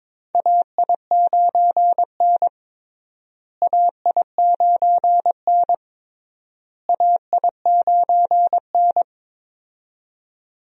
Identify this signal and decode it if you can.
Morse code, heard as a keyed tone, about 22 wpm, 700 Hz. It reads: AI9N AI9N AI9N
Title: USA Call Signs - 4 Characters - Repeated 3x - 22wpm